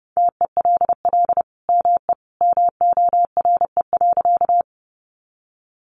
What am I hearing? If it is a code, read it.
Morse code, heard as a keyed tone, about 30 wpm, 700 Hz. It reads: TELL ME MORE.
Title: Sentences from Top 300 Words 30wpm